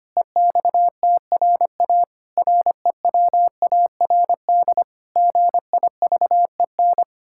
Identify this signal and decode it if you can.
Morse code, heard as a keyed tone, about 25 wpm, 700 Hz. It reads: EXTRA REWARD GIVEN
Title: Pangram Challenge v2 - 25wpm